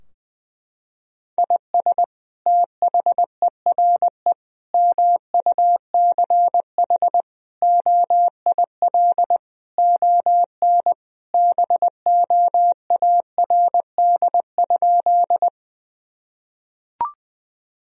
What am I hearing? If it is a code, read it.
Morse code, heard as a keyed tone, about 21 wpm, 700 Hz. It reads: IS THERE MUCH OIL ON BOARD?
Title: Sentences from Top 1000 Words - Review Part 4 - 20wpm